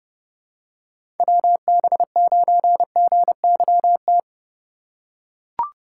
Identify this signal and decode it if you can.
Morse code, heard as a keyed tone, about 30 wpm, 700 Hz. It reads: WB9GYT
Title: USA Call Signs - 6 Characters - 3x Character Spacing - 30wpm